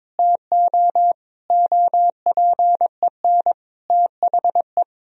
Morse code, heard as a keyed tone, about 21 wpm, 700 Hz. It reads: TO OPEN THE
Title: Sentences from Top 1000 Words - Review Part 1 - 22wpm